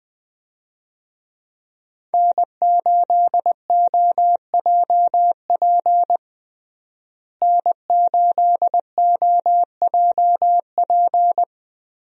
Morse code, heard as a keyed tone, about 21 wpm, 700 Hz. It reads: N8OJP N8OJP
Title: USA Call Signs - 5 Characters - Repeated 3x - 20wpm